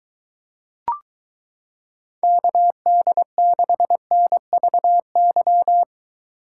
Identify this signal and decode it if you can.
Morse code, heard as a keyed tone, about 22 wpm, 700 Hz. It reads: KD6NVY